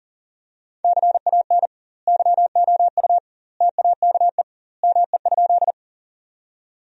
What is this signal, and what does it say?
Morse code, heard as a keyed tone, about 40 wpm, 700 Hz. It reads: CAN YOU TAKE ME?
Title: 2x Word-Spacing - Sentences from Top 200 Words - 40wpm